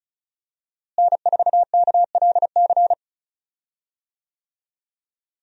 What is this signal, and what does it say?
Morse code, heard as a keyed tone, about 34 wpm, 700 Hz. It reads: N4KLC
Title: USA Call Signs - 5 Characters - Repeated 2x - 35wpm